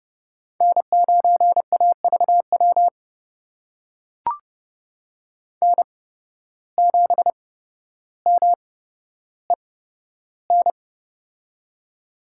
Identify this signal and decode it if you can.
Morse code, heard as a keyed tone, about 30 wpm, spaced as for 13 wpm, 700 Hz. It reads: N9AVW N7MEN
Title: USA Call Signs - 5 Characters - 8x Character Spacing - 30wpm